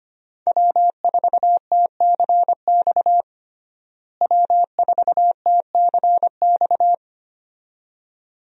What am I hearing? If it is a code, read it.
Morse code, heard as a keyed tone, about 25 wpm, 700 Hz. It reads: W4TCX W4TCX
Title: USA Call Signs - 5 Characters - Repeated 2x - 25wpm